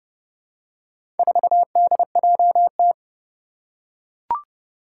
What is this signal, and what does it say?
Morse code, heard as a keyed tone, about 30 wpm, 700 Hz. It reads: 4DJT
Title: Four Challenging Letter-Number Combinations 30wpm